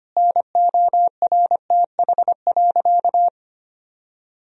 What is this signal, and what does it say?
Morse code, heard as a keyed tone, about 25 wpm, 700 Hz. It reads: NORTH.
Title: Sentences from Top 1000 Words - Review Part 1 - 25wpm